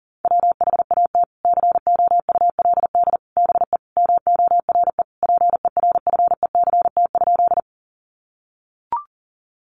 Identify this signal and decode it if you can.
Morse code, heard as a keyed tone, about 40 wpm, 700 Hz. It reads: WHAT COULD BE MORE PERFECT?
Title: Sentences from Top 1000 Words 40wpm